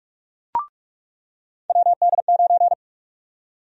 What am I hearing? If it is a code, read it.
Morse code, heard as a keyed tone, about 46 wpm, 700 Hz. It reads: WD9
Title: USA Call Sign Prefixes - 45wpm